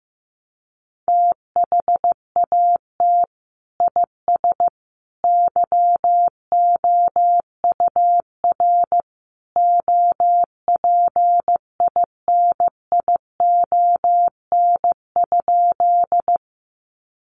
Morse code, heard as a keyed tone, about 15 wpm, 700 Hz. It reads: THAT IS YOUR OPINION?
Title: Sentences from Top 1200 Words 15wpm